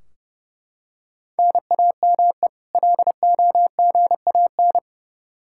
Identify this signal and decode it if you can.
Morse code, heard as a keyed tone, about 29 wpm, 700 Hz. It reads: NAME LOGAN
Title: Practice QSO - Exchange 1 - Beginning - 30wpm